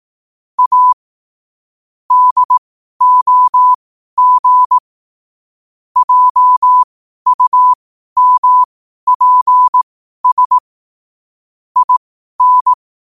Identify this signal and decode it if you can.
Morse code, heard as a keyed tone, about 18 wpm, spaced as for 12 wpm, 1000 Hz. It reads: A DOG JUMPS IN
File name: qotdfiles / 12wpm